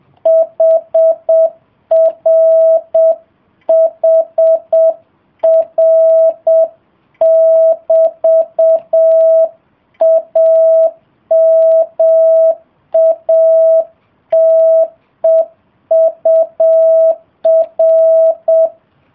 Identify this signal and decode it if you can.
Morse code, heard as a keyed tone, about 7 wpm, 650 Hz. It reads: HRHR<BT>AMATEUR